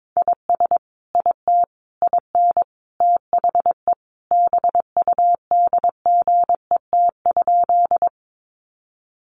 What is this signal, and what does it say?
Morse code, heard as a keyed tone, about 22 wpm, 700 Hz. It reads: IS IT IN THE BUDGET?
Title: Sentences from Top 1200 Words 22wpm